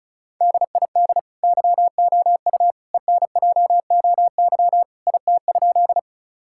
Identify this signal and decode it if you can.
Morse code, heard as a keyed tone, about 36 wpm, 700 Hz. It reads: DID YOU ENJOY IT?
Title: Sentences from Top 1000 Words - Review Part 5 - 35wpm